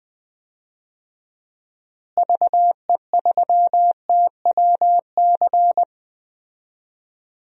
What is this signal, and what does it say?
Morse code, heard as a keyed tone, about 20 wpm, 700 Hz. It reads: VE3TWC
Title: DXCC-100 Call Signs - 6 Characters - Repeated 3x - 20wpm